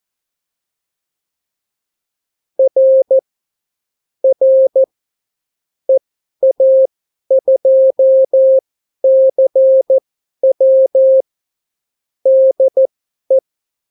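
Morse code, heard as a keyed tone, about 14 wpm, spaced as for 11 wpm, 530 Hz. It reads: R R EA2CW DE
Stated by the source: Example Spanish CW QSO from a Morse learning book, 14 wpm characters with 8 wpm spacing.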